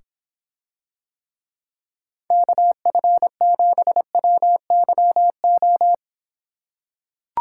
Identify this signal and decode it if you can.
Morse code, heard as a keyed tone, about 26 wpm, 700 Hz. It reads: KF7WYO